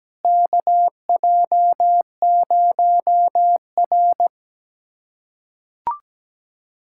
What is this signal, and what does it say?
Morse code, heard as a keyed tone, about 17 wpm, 700 Hz. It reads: KJ0R
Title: USA Callsign DE Callsign - 4 Characters - 17wpm